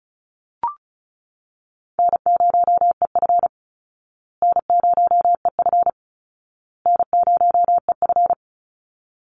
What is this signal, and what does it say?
Morse code, heard as a keyed tone, about 34 wpm, 700 Hz. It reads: N0EF N0EF N0EF
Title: Call Signs Repeated 2x - 35wpm